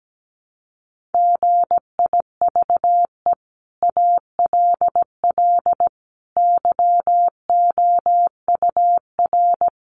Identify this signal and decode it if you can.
Morse code, heard as a keyed tone, about 17 wpm, 700 Hz. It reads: GIVE ALL YOUR
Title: Sentences from Top 600 Words 17wpm